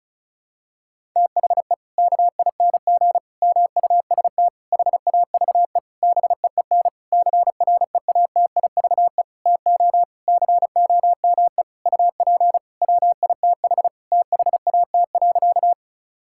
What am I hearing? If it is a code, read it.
Morse code, heard as a keyed tone, about 36 wpm, 700 Hz. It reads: THE KING MUST HAVE BEEN CREATIVE TO COME UP WITH THAT.
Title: Sentences from Top 1000 Words - Review Part 5 - 35wpm